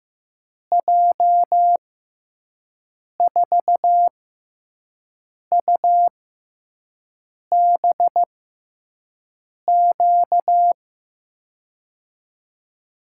Morse code, heard as a keyed tone, about 15 wpm, 700 Hz. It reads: J 4 U B Q
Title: USA Call Signs - 6x Character Spacing - 15wpm